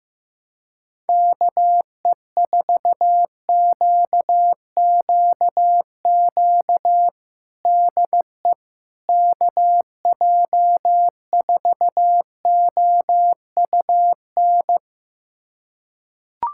Morse code, heard as a keyed tone, about 15 wpm, 700 Hz. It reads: KE4QQQ DE KJ4OUN
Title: USA Callsign DE Callsign - 6 Characters - 15wpm